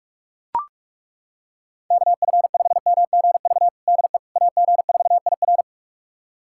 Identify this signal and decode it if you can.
Morse code, heard as a keyed tone, about 41 wpm, 700 Hz. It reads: KF5MGV DE AG4IR
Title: USA Callsign DE Callsign 45wpm